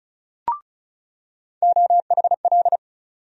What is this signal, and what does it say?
Morse code, heard as a keyed tone, about 36 wpm, 700 Hz. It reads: OHL
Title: USA Call Sign Postfixes - 3 Letters 35wpm